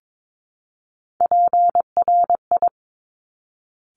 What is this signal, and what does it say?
Morse code, heard as a keyed tone, about 22 wpm, 700 Hz. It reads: PRI
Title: Top Three Letter Combinations in Words 22wpm